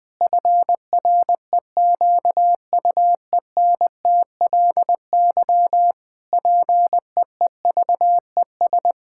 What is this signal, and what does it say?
Morse code, heard as a keyed tone, about 21 wpm, 700 Hz. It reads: FREQUENTLY PEEVES
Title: Pangram Challenge v2 - 20wpm